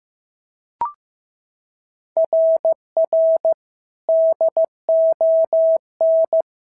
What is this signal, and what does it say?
Morse code, heard as a keyed tone, about 14 wpm, 650 Hz. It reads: RR DON
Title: SOTA QSO Practice 15wpm